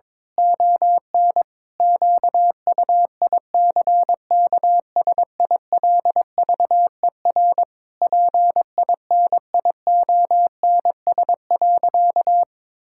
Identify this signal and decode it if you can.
Morse code, heard as a keyed tone, about 22 wpm, 700 Hz. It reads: ON QUICKSILVER PINIONS.
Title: Pangram Challenge v2 - 22wpm